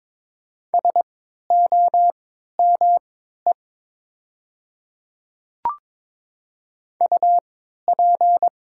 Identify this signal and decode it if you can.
Morse code, heard as a keyed tone, about 22 wpm, spaced as for 13 wpm, 700 Hz. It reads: SOME UP
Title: 3x Character Spacing - Top 100 Words - 22wpm